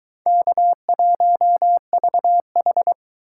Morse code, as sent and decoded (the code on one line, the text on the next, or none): -.- .---- ...- ....
K1VH